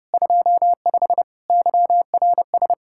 Morse code, heard as a keyed tone, about 36 wpm, 700 Hz. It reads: 25 YRS
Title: Speed-Racing Common QSO Phrases - 30wpm